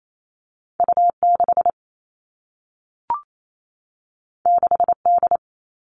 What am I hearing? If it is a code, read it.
Morse code, heard as a keyed tone, about 28 wpm, 700 Hz. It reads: U6 6D
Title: All Cut Numbers - 2-Digit 28wpm